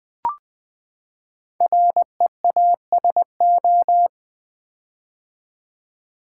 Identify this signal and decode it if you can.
Morse code, heard as a keyed tone, about 20 wpm, 700 Hz. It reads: REASO
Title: Build A Word - 5 Letters At A Time - 10-Letter Words - 20wpm